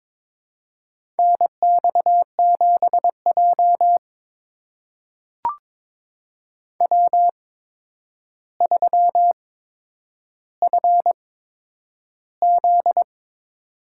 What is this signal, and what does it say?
Morse code, heard as a keyed tone, about 22 wpm, spaced as for 9 wpm, 700 Hz. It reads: NX7J W3FZ
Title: USA Call Signs - 4 Characters - 8x Character Spacing - 22wpm